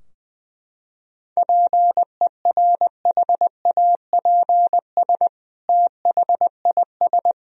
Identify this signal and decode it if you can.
Morse code, heard as a keyed tone, about 20 wpm, 700 Hz. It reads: PERHAPS THIS
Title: Sentences from Top 1000 Words - Review Part 2 - 20wpm